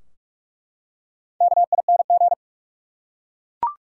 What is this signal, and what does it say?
Morse code, heard as a keyed tone, about 43 wpm, 700 Hz. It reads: KING
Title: Four Letter Words - 45wpm